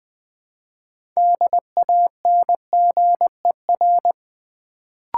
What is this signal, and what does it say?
Morse code, heard as a keyed tone, about 20 wpm, 700 Hz. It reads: DANGER
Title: Build A Word - 3 Letters At A Time - 6-Letter Words - 20wpm